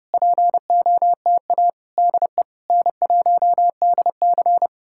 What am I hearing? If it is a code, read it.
Morse code, heard as a keyed tone, about 33 wpm, 700 Hz. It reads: POTA DE N1DC